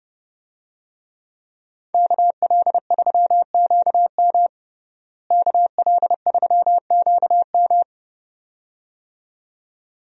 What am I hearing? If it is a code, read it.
Morse code, heard as a keyed tone, about 30 wpm, 700 Hz. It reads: KL3QM KL3QM
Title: USA Call Signs - 5 Characters - Repeated 2x - 30wpm